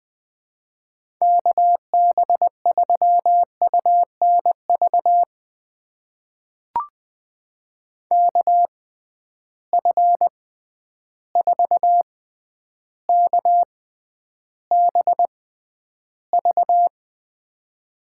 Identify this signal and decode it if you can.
Morse code, heard as a keyed tone, about 20 wpm, spaced as for 10 wpm, 700 Hz. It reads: KB3UNV KF4KBV
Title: USA Call Signs - 6 Characters - 6x Character Spacing - 20wpm